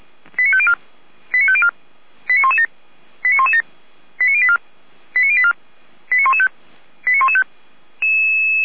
Five-tone selective call (ZVEI), heard as quick runs of five tones, tones of about 70 ms. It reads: 89504 89504 89117 89117 89084 89084 89115 89115